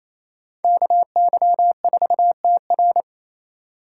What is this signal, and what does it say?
Morse code, heard as a keyed tone, about 35 wpm, 700 Hz. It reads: KY4TR